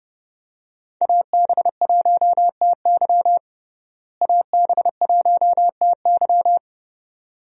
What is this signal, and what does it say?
Morse code, heard as a keyed tone, about 30 wpm, 700 Hz. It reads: AB1TY AB1TY
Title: USA Call Signs - 5 Characters - Repeated 2x - 30wpm